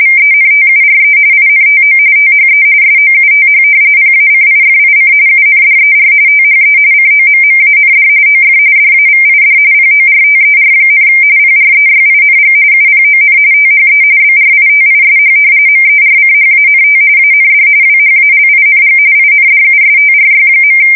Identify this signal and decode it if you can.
RTTY, about 45.45 baud. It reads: VENTURE INTO THE SOUTH PACIFIC UNDER THE COMMAND OF DAVID PORTER, PREYING UPON THE BRITISH MERCHANT AND WHALING INDUSTRY.
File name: fsk170 / usnavy2